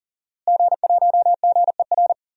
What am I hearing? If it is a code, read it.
Morse code, heard as a keyed tone, about 40 wpm, 700 Hz. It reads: G1GER
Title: DXCC-100 Call Signs - 5 Characters - Repeated 3x - 40wpm